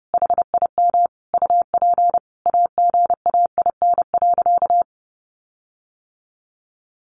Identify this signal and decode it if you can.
Morse code, heard as a keyed tone, about 30 wpm, 700 Hz. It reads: HIM UP AGAIN.
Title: Sentences from Top 1400 Words 30wpm